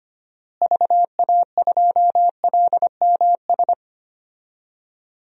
Morse code, heard as a keyed tone, about 25 wpm, 700 Hz. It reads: VA2LMS
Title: DXCC-100 Call Signs - 6 Characters - Repeated 2x - 25wpm